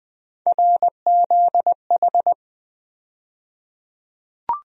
Morse code, heard as a keyed tone, about 20 wpm, 700 Hz. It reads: RZH
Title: USA Call Sign Postfixes - 3 Letters 20wpm